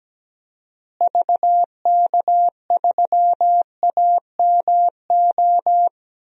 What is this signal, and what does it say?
Morse code, heard as a keyed tone, about 17 wpm, 700 Hz. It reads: VK3AMO
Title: DXCC-100 Call Signs - 6 Characters - Repeated 3x - 17wpm